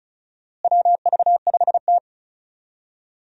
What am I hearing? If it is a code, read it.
Morse code, heard as a keyed tone, about 36 wpm, 700 Hz. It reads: WV5T
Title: USA Call Signs - 4 Characters - Repeated 3x - 35wpm